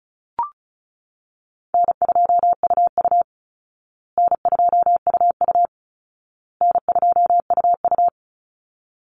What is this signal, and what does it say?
Morse code, heard as a keyed tone, about 34 wpm, 700 Hz. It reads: N2UU N2UU N2UU
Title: Easy Call Signs Repeated 3x - 35wpm